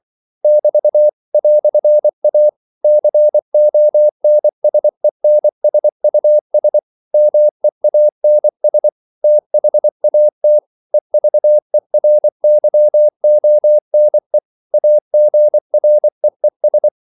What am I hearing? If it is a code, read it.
Morse code, heard as a keyed tone, about 24 wpm, 600 Hz. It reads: <BT> "A CONSENSUS MEANS THAT EVERYONE AGREES
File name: CW / Practice / EnglishQuotations0424